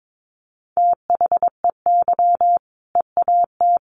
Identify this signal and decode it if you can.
Morse code, heard as a keyed tone, about 22 wpm, 700 Hz. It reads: THEY EAT
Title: Easy Phrases 22wpm